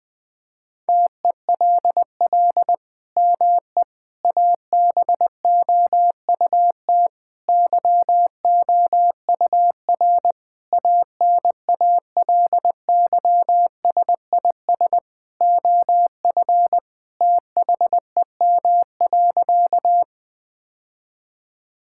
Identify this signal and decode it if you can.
Morse code, heard as a keyed tone, about 20 wpm, 700 Hz. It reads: TELL ME ABOUT YOUR ANALYSIS OF THEM.
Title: Sentences from Top 1000 Words - Review Part 5 - 20wpm